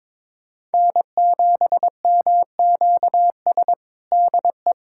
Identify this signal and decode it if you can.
Morse code, heard as a keyed tone, about 22 wpm, 700 Hz. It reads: N7MQS DE
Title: USA Callsign DE Callsign 22wpm